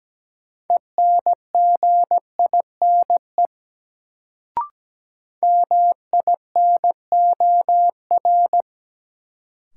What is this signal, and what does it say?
Morse code, heard as a keyed tone, about 17 wpm, 700 Hz. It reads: ENGINE MINOR
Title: Sentences from Top 1500 Words - New Word Drill Alt - 17wpm